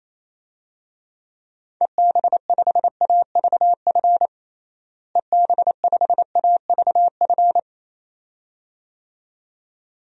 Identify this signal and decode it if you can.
Morse code, heard as a keyed tone, about 28 wpm, 700 Hz. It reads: EB5AVF EB5AVF
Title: DXCC-100 Call Signs - 6 Characters - Repeated 2x - 28wpm